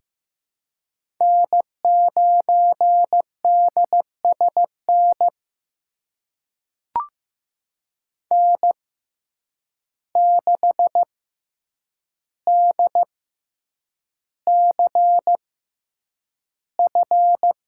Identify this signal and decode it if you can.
Morse code, heard as a keyed tone, about 15 wpm, spaced as for 8 wpm, 700 Hz. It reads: N9DSN N6DCF
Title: USA Call Signs - 5 Characters - 6x Character Spacing - 15wpm